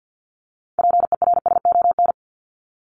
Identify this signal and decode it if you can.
Morse code, heard as a keyed tone, about 50 wpm, 700 Hz. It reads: PERSON
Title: ES:S Suffix - Top 100 Words with Base Words 50wpm